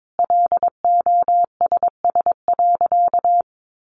Morse code, heard as a keyed tone, about 22 wpm, 700 Hz. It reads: LOSS.
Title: Sentences from Top 1400 Words 22wpm